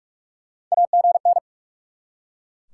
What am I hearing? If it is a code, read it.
Morse code, heard as a keyed tone, about 44 wpm, 690 Hz. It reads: AGN
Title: SOTA QSO Practice 45wpm